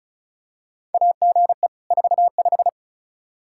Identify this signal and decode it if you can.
Morse code, heard as a keyed tone, about 36 wpm, 700 Hz. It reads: AGE 45
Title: Practice QSO - Exchange 3 - More Info - 35wpm